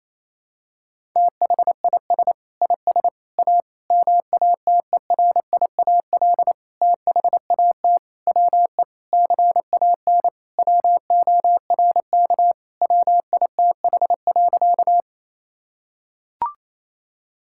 Morse code, heard as a keyed tone, about 28 wpm, 700 Hz. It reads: THIS IS A MATERIAL THAT WE CAN WORK WITH.
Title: Sentences from Top 1000 Words - Review Part 5 - 28wpm